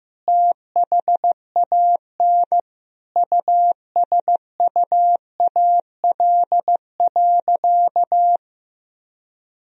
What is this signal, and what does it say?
Morse code, heard as a keyed tone, about 15 wpm, 700 Hz. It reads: THAN USUAL.
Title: Sentences from Top 1000 Words - Review Part 2 - 15wpm